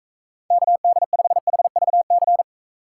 Morse code, heard as a keyed tone, about 36 wpm, 700 Hz. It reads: KD5HVC